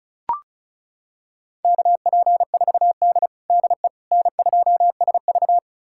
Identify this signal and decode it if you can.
Morse code, heard as a keyed tone, about 36 wpm, 700 Hz. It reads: KP4D DE N2SV
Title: USA Callsign DE Callsign - 4 Characters - 35wpm